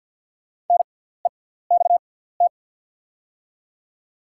Morse code, heard as a keyed tone, about 48 wpm, spaced as for 17 wpm, 700 Hz. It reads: NEXT
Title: 6x Character Spacing - Top 300 Words - 50wpm